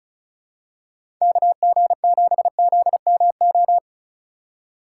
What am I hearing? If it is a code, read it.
Morse code, heard as a keyed tone, about 34 wpm, 700 Hz. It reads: KG7ZMO
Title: USA Call Signs - 6 Characters - 2x Character Spacing - 35wpm